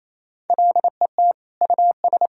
Morse code, heard as a keyed tone, about 27 wpm, 700 Hz. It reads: LET US
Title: Sentences from Top 1000 Words - Review Part 2 - 28wpm